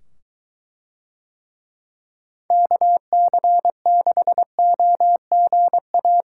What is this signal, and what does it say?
Morse code, heard as a keyed tone, about 21 wpm, 700 Hz. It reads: KC6OGA